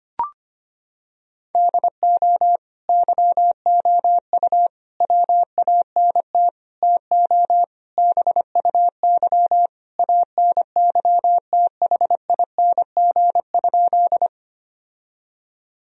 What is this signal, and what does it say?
Morse code, heard as a keyed tone, about 25 wpm, 700 Hz. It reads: DO YOU WANT TO BUY ANYTHING?
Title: Sentences from Top 1000 Words - Review Part 4 - 25wpm